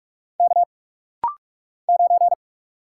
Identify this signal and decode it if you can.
Morse code, heard as a keyed tone, about 47 wpm, 700 Hz. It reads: K 9
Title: Single Letter-Number - Rapid-Fire 45wpm